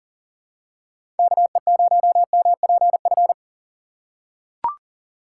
Morse code, heard as a keyed tone, about 40 wpm, 700 Hz. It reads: KE0MPF
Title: USA Call Signs - 6 Characters - 2x Character Spacing - 40wpm